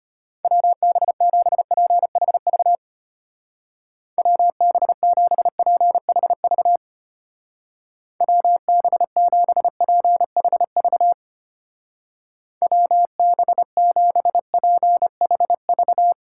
Morse code, heard as a keyed tone, about 33 wpm, 700 Hz. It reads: WB7PHV WB7PHV WB7PHV WB7PHV